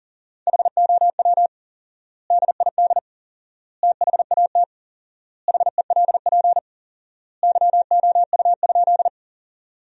Morse code, heard as a keyed tone, about 40 wpm, 700 Hz. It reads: HOW DID THAT HELP YOU?
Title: 4x Word-Spacing - Sentences from Top 200 Words - 40wpm